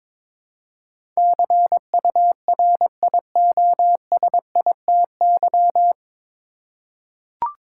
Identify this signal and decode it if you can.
Morse code, heard as a keyed tone, about 22 wpm, 700 Hz. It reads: CURIOSITY
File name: 9 Letter Words - First 3 Letters Given - 22wpm